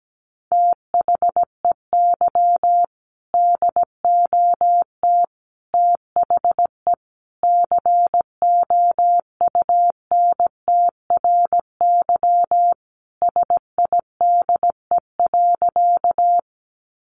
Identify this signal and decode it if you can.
Morse code, heard as a keyed tone, about 17 wpm, 700 Hz. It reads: THEY DOT THE COUNTRY SIDE.
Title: Sentences from Top 1500 Words 17wpm